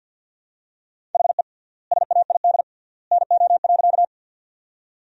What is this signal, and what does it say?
Morse code, heard as a keyed tone, about 50 wpm, 700 Hz. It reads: HE SAID NO.
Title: 3x Word-Spacing - Sentences from Top 100 Words - 50wpm